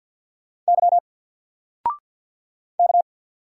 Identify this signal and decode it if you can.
Morse code, heard as a keyed tone, about 49 wpm, 700 Hz. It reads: Y K
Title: Single Character (Letter:Number:Punctuation:Prosign) 50wpm